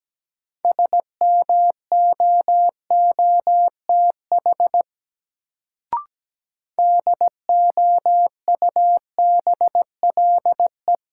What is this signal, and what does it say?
Morse code, heard as a keyed tone, about 17 wpm, 700 Hz. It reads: SMOOTH DOUBLE
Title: Sentences from Top 1400 Words - New Word Drill Alt - 17wpm